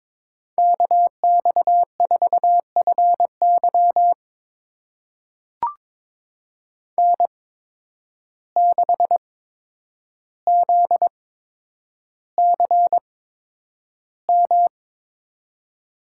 Morse code, heard as a keyed tone, about 22 wpm, spaced as for 9 wpm, 700 Hz. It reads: KX4FY N6ZCM
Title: USA Call Signs - 5 Characters - 8x Character Spacing - 22wpm